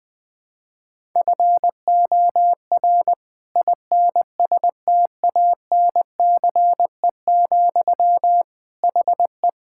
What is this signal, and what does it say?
Morse code, heard as a keyed tone, about 20 wpm, 700 Hz. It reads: FOR INSTANCE, HE
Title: Sentences from Top 1000 Words - Review Part 5 - 20wpm